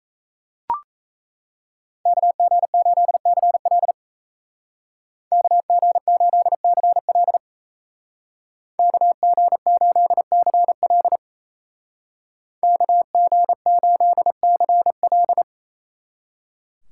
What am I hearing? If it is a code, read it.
Morse code, heard as a keyed tone, about 34 wpm, 700 Hz. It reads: KG8CL KG8CL KG8CL KG8CL
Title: Speed Racing - USA 2x2 Call Signs 28wpm